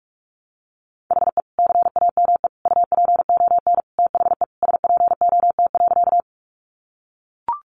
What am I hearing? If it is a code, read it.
Morse code, heard as a keyed tone, about 44 wpm, 700 Hz. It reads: HE CAME UPON THE SPOT.
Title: Sentences from Top 1100 Words 45wpm